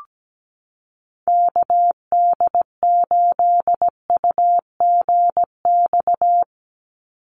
Morse code, heard as a keyed tone, about 17 wpm, 700 Hz. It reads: KD8UGX
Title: USA Call Signs 17wpm